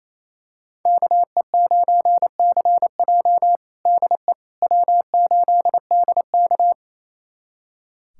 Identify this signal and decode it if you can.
Morse code, heard as a keyed tone, about 28 wpm, 700 Hz. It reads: KE9CJ DE W8DK
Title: USA Callsign DE Callsign 28wpm